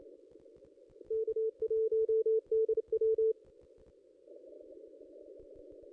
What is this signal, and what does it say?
Morse code, heard as a keyed tone, about 28 wpm, 440 Hz. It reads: K1DW
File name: contests / ss-cw-k5tr-op-n5rz -2200